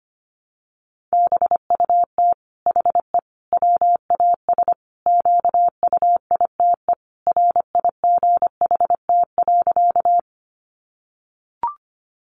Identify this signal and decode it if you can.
Morse code, heard as a keyed tone, about 25 wpm, 700 Hz. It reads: BUT HE WAS QUITE RIGHT.
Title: Sentences from Top 700 Words 25wpm